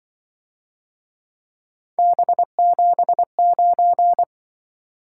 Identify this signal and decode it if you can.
Morse code, heard as a keyed tone, about 24 wpm, 700 Hz. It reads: B79
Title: Speed-Racing - Three Letter-Number Combinations - 20wpm